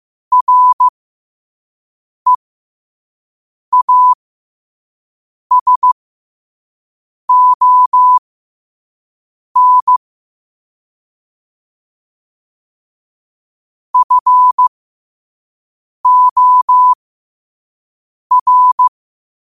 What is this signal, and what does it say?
Morse code, heard as a keyed tone, about 15 wpm, spaced as for 5 wpm, 1000 Hz. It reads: REASON FOR
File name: qotdfiles / 5wpm